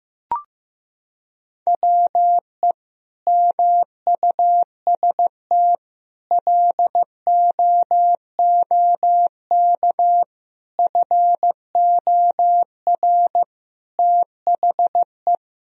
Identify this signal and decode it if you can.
Morse code, heard as a keyed tone, about 15 wpm, 700 Hz. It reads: WE MUST LOOK FOR THE
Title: Sentences from Top 1000 Words - Review Part 2 - 15wpm